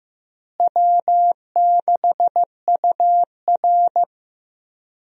Morse code, heard as a keyed tone, about 15 wpm, 700 Hz. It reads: W6UR
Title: USA Call Signs - 4 Characters - Repeated 3x - 15wpm